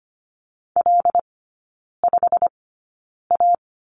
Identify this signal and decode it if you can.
Morse code, heard as a keyed tone, about 25 wpm, spaced as for 9 wpm, 700 Hz. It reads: L5A
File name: course / 25:10wpm / 093 Lesson 16e - Letter c - Three Characters 25wpm